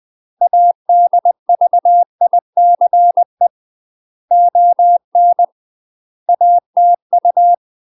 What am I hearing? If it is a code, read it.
Morse code, heard as a keyed tone, about 20 wpm, 700 Hz. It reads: ADVICE ON ATU
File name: UK MASTER QSOs TRAINING / 20WPM / UK MASTER QSOs 0055